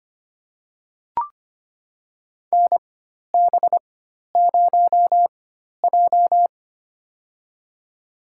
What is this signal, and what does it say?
Morse code, heard as a keyed tone, about 25 wpm, spaced as for 12 wpm, 700 Hz. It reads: NB0J
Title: USA Call Signs - 4 Characters - 4x Character Spacing - 25wpm